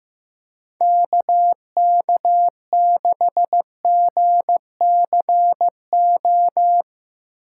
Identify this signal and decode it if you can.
Morse code, heard as a keyed tone, about 15 wpm, 700 Hz. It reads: KK6GCO
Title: USA Call Signs - 6x Character Spacing - 15wpm